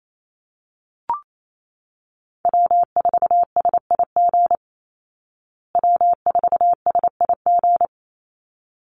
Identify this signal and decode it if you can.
Morse code, heard as a keyed tone, about 28 wpm, 700 Hz. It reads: W4SIG W4SIG
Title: Call Signs Repeated 3x - 28wpm